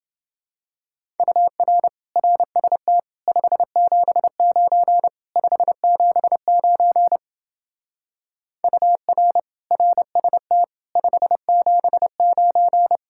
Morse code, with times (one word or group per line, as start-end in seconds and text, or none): 1.2-1.9 UR
2.2-3.0 RST
3.3-5.1 579
5.4-7.2 579
8.6-9.4 UR
9.7-10.6 RST
11.0-13.0 579